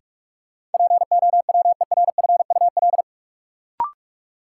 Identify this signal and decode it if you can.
Morse code, heard as a keyed tone, about 46 wpm, 700 Hz. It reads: POWERFUL
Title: Sentences from Top 1000 Words - Review Part 6 - New Word Drill - 45wpm